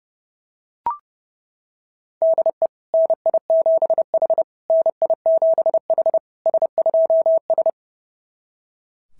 Morse code, heard as a keyed tone, about 26 wpm, 650 Hz. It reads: DE NI7H NI7H S2S
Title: SOTA QSO Practice 30wpm